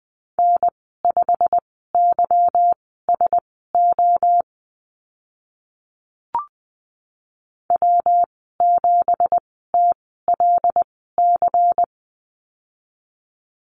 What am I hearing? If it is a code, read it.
Morse code, heard as a keyed tone, about 20 wpm, spaced as for 15 wpm, 700 Hz. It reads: N5YSO W7TLC
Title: 179 Lesson 29g - Slash - Call Signs 20wpm